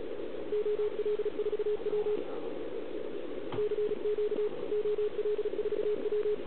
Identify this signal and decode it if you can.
Morse code, heard as a keyed tone, about 36 wpm, 410 Hz, with radio noise. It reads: OL4W CQ OL4W